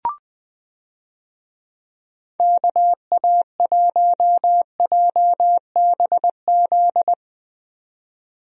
Morse code, heard as a keyed tone, about 21 wpm, 700 Hz. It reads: KA1JBZ
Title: USA Call Signs - 6 Characters - 20wpm